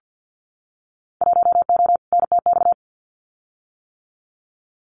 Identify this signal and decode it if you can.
Morse code, heard as a keyed tone, about 50 wpm, 700 Hz. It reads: 1O NTX
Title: Winter Field Day Exchanges 50wpm